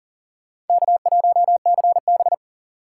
Morse code, heard as a keyed tone, about 51 wpm, 700 Hz. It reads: K1CB